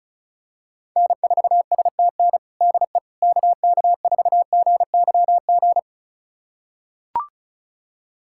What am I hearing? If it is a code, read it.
Morse code, heard as a keyed tone, about 36 wpm, 700 Hz. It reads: N4STN DE KK4GYG
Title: USA Callsign DE Callsign 35wpm